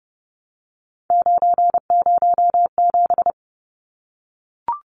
Morse code, heard as a keyed tone, about 30 wpm, 700 Hz. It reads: 907
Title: Numbers - 3-Digit 30wpm